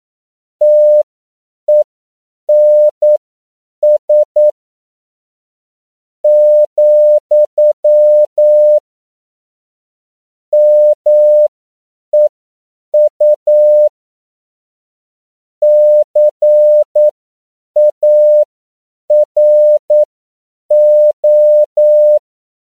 Morse code, heard as a keyed tone, about 9 wpm, spaced as for 7 wpm, 600 Hz. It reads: TENS, MEU CARO